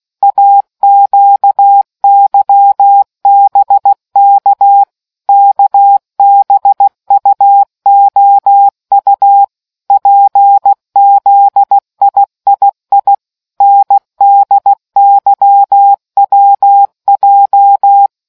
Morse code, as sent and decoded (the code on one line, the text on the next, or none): .- --.- -.-- -... -.- / -.- -... ..- --- ..- / .--. --.. .. .. .. / -. -.. -.-- .-- .---
AQYBK KBUOU PZIII NDYWJ